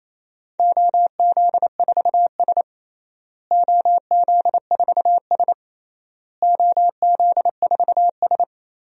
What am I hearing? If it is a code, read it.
Morse code, heard as a keyed tone, about 28 wpm, 700 Hz. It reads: OZ4S OZ4S OZ4S
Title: DXCC-100 Call Signs - 4 Characters - Repeated 3x - 28wpm